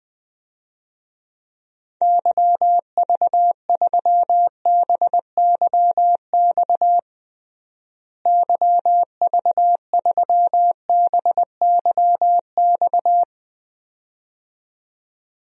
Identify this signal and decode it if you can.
Morse code, heard as a keyed tone, about 21 wpm, 700 Hz. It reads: YV3BYX YV3BYX
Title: DXCC-100 Call Signs - 6 Characters - Repeated 2x - 20wpm